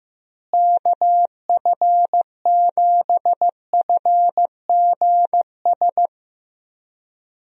Morse code, heard as a keyed tone, about 15 wpm, 700 Hz. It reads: KF7FGS
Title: USA Call Signs - 6 Characters - Repeated 3x - 15wpm